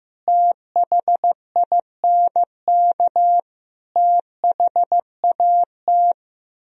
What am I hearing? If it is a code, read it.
Morse code, heard as a keyed tone, about 15 wpm, 700 Hz. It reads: THINK THAT
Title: Speed-Racing - Sets of 5 Words 15wpm